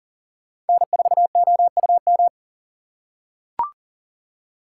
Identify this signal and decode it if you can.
Morse code, heard as a keyed tone, about 37 wpm, 700 Hz. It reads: N4OUM
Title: USA Call Signs - 4x Character Spacing - 40wpm